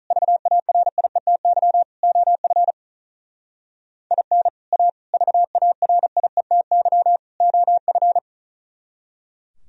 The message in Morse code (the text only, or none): VARIETY OF IN A VARIETY OF